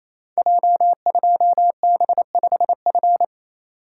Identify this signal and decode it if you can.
Morse code, heard as a keyed tone, about 28 wpm, 700 Hz. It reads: J2B5F
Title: Code Groups - 5 Letter:Numbers - 28wpm